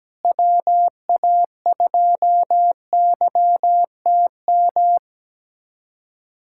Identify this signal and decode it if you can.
Morse code, heard as a keyed tone, about 17 wpm, 700 Hz. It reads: WA2YTM
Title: USA Callsign DE Callsign - 6 Characters - 17wpm